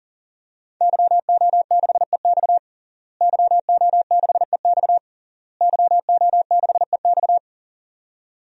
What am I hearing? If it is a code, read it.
Morse code, heard as a keyed tone, about 40 wpm, 700 Hz. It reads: YO6EX YO6EX YO6EX
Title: DXCC-100 Call Signs - 5 Characters - Repeated 3x - 40wpm